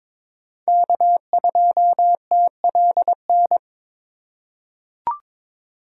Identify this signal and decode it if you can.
Morse code, heard as a keyed tone, about 22 wpm, 700 Hz. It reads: K2TLN
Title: USA Call Signs - 2x Character Spacing - 22wpm